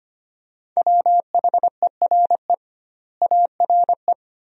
Morse code, heard as a keyed tone, about 25 wpm, 700 Hz. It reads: WHERE ARE
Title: 2x Word-Spacing - Sentences from Top 200 Words - 25wpm